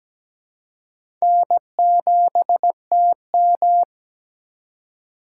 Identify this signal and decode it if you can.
Morse code, heard as a keyed tone, about 17 wpm, 700 Hz. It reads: N7TM
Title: USA Call Signs - 4x Character Spacing - 17wpm